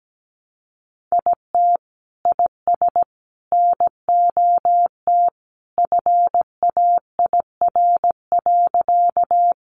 Morse code, heard as a keyed tone, about 17 wpm, 700 Hz. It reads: IT IS NOT FAIR.
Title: Sentences from Top 1300 Words 17wpm